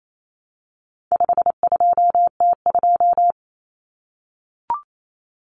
Morse code, heard as a keyed tone, about 28 wpm, 700 Hz. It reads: 52T2
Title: Common Cut Numbers - 4-Digit 28wpm